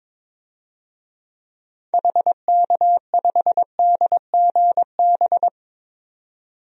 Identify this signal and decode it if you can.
Morse code, heard as a keyed tone, about 22 wpm, 700 Hz. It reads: HK5DGB
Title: DXCC-100 Call Signs - 6 Characters - 22wpm